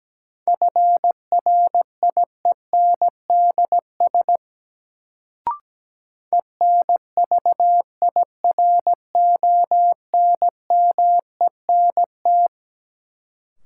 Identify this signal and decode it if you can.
Morse code, heard as a keyed tone, about 17 wpm, 700 Hz. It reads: FRIENDS ENVIRONMENT
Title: Sentences from Top 1000 Words - Review Part 5 - New Word Drill - 17wpm